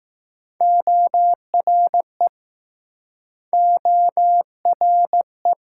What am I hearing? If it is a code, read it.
Morse code, heard as a keyed tone, about 22 wpm, 700 Hz. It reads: ORE ORE